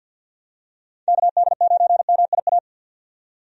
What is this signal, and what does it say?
Morse code, heard as a keyed tone, about 52 wpm, 700 Hz. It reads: KD9MIA
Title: USA Call Signs - 6 Characters - 50wpm